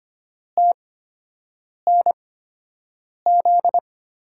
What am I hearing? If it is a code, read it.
Morse code, heard as a keyed tone, about 25 wpm, spaced as for 7 wpm, 700 Hz. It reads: TNZ
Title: USA Call Signs - 5 Characters - 8x Character Spacing - 25wpm